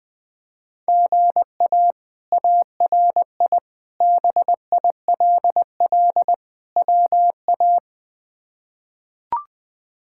Morse code, heard as a keyed tone, about 20 wpm, 700 Hz. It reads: GA ARI BILL WA
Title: K1USN SST QSO Practice - 20wpm